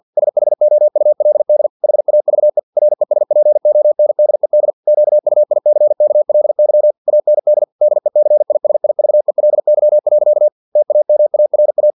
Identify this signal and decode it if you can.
Morse code, heard as a keyed tone, about 50 wpm, 600 Hz. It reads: SHOULD HAVE RESPONDED QUICKLY AND DECISIVELY. TAMARA
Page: Never 49wpm